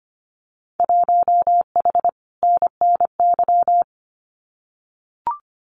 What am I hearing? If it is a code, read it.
Morse code, heard as a keyed tone, about 25 wpm, 700 Hz. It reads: 1H NNY
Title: Winter Field Day Exchanges 25wpm